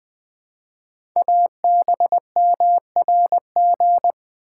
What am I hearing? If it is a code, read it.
Morse code, heard as a keyed tone, about 20 wpm, 700 Hz. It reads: ABMRG
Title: Code Groups - 5 Letters - 20wpm